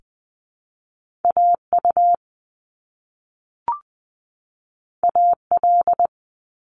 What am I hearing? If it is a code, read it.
Morse code, heard as a keyed tone, about 20 wpm, 700 Hz. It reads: AU AL
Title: Top Two Letter Combinations in Words 20wpm